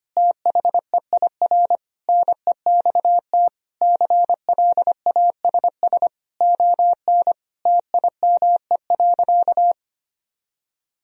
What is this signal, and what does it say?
Morse code, heard as a keyed tone, about 25 wpm, 700 Hz. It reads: THEIR NEXT CLASS ON TIME.
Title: Sentences from Top 1000 Words - Review Part 2 - 25wpm